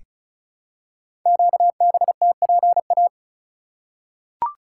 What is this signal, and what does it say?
Morse code, heard as a keyed tone, about 36 wpm, 700 Hz. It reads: QBTPA
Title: Code Groups - 5 Letters - 35wpm